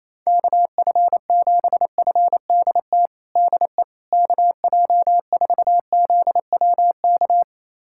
Morse code, heard as a keyed tone, about 28 wpm, 700 Hz. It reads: KF7FDT DE KJ4ZWK
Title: USA Callsign DE Callsign - 6 Characters - 28wpm